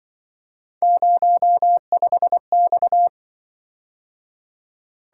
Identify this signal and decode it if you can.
Morse code, heard as a keyed tone, about 24 wpm, 700 Hz. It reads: 05X
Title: Speed-Racing - Three Letter-Number Combinations - 20wpm